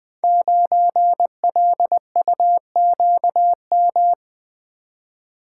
Morse code, heard as a keyed tone, about 20 wpm, 700 Hz. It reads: 9LUQM
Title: Code Groups - 5 Letter:Numbers - 20wpm